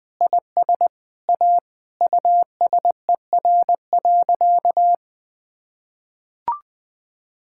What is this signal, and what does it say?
Morse code, heard as a keyed tone, about 20 wpm, 700 Hz. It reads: IS A USER.
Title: Sentences from Top 1000 Words - Review Part 5 - 20wpm